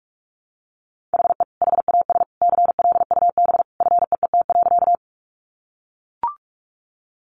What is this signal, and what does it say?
Morse code, heard as a keyed tone, about 44 wpm, 700 Hz. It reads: HE HAS CLUB FEET.
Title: Sentences from Top 1200 Words 45wpm